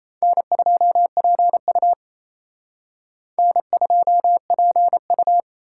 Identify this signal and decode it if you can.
Morse code, heard as a keyed tone, about 35 wpm, 700 Hz. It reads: N2PU N2PU
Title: Speed Racing - USA Call Signs 28wpm